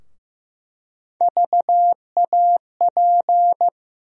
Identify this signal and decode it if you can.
Morse code, heard as a keyed tone, about 15 wpm, 700 Hz. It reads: VAP
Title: USA Call Sign Postfixes - 3 Letters 15wpm